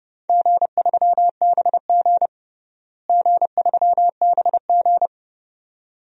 Morse code, heard as a keyed tone, about 30 wpm, 700 Hz. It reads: G3BG G3BG
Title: DXCC-100 Call Signs - 4 Characters - Repeated 2x - 30wpm